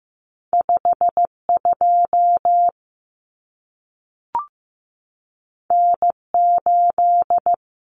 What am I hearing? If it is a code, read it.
Morse code, heard as a keyed tone, about 15 wpm, 700 Hz. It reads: 52 N8
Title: Common Cut Numbers - 2-Digit 15wpm